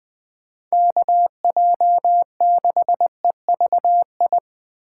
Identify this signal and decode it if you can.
Morse code, heard as a keyed tone, about 20 wpm, 700 Hz. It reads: KJ6EVI
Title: USA Call Signs - 3x Character Spacing - 20wpm